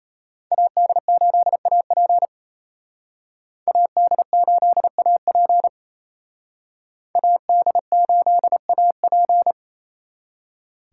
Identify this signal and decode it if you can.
Morse code, heard as a keyed tone, about 32 wpm, 700 Hz. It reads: AD8AP AD8AP AD8AP
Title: Speed Racing - USA 2x2 Call Signs 28wpm